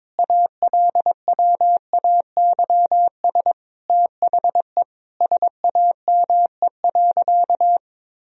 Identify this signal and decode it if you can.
Morse code, heard as a keyed tone, about 22 wpm, 700 Hz. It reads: ALWAYS THE SAME.
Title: Sentences from Top 300 Words and Abbreviations 22wpm